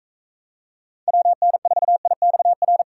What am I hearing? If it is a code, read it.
Morse code, heard as a keyed tone, about 42 wpm, 700 Hz. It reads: WN4IXR